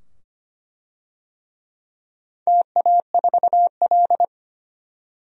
Morse code, heard as a keyed tone, about 25 wpm, 700 Hz. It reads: TA4L
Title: DXCC-100 Call Signs - 4 Characters - 25wpm